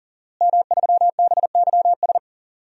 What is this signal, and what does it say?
Morse code, heard as a keyed tone, about 37 wpm, 700 Hz. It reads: M3BYS
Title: DXCC-100 Call Signs - 5 Characters - Repeated 2x - 40wpm